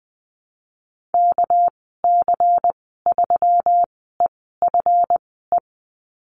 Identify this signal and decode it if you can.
Morse code, heard as a keyed tone, about 20 wpm, spaced as for 15 wpm, 700 Hz. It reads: KC3EFE
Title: 213 Lesson 34g - Number 8 - Call Signs 20wpm